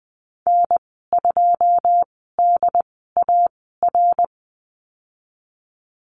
Morse code, heard as a keyed tone, about 20 wpm, spaced as for 15 wpm, 700 Hz. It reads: N2DAR
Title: 088 Lesson 15g - Number 5 - Call Signs 20wpm